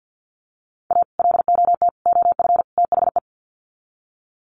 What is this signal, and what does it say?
Morse code, heard as a keyed tone, about 50 wpm, 700 Hz. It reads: A LOT OF THE
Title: Sets of 4 Words 50wpm